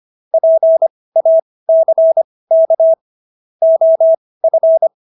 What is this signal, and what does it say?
Morse code, heard as a keyed tone, about 25 wpm, spaced as for 18 wpm, 650 Hz. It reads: PACK OF